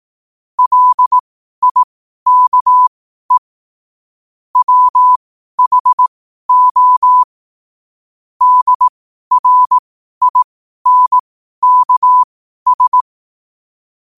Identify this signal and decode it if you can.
Morse code, heard as a keyed tone, about 18 wpm, spaced as for 12 wpm, 1000 Hz. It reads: LIKE WHO DRINKS
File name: qotdfiles / 12wpm